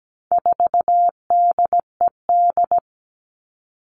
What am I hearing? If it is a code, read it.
Morse code, heard as a keyed tone, about 17 wpm, 700 Hz. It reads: 4DED
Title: All Cut Numbers - 4-Digit 17wpm